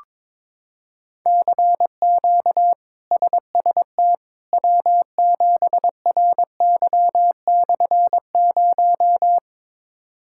Morse code, heard as a keyed tone, about 22 wpm, 700 Hz. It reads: CQ SST W7RY/0
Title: K1USN SST QSO Practice - 22wpm